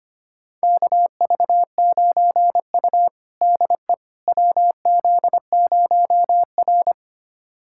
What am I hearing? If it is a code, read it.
Morse code, heard as a keyed tone, about 25 wpm, 700 Hz. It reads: KV9U DE WZ0R
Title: USA Callsign DE Callsign - 4 Characters - 25wpm